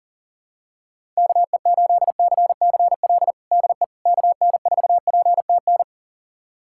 Morse code, heard as a keyed tone, about 37 wpm, 700 Hz. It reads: KE8CCL DE KN4PTN
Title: USA Callsign DE Callsign - 6 Characters - 40wpm